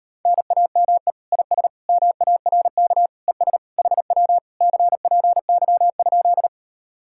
Morse code, heard as a keyed tone, about 38 wpm, 700 Hz. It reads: NAME IS MARK ES HW CPY?